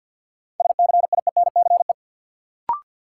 Morse code, heard as a keyed tone, about 53 wpm, 700 Hz. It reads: SCIENCE
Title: Sentences from Top 1000 Words - Review Part 2 - New Word Drill - 50wpm